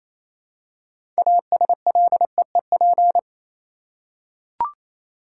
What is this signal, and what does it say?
Morse code, heard as a keyed tone, about 27 wpm, 700 Hz. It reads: ASLEEP
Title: Six Letter Words - 28wpm